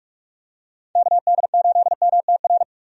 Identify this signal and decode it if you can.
Morse code, heard as a keyed tone, about 46 wpm, 700 Hz. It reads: KD8MTR
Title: USA Call Signs - 6 Characters - 45wpm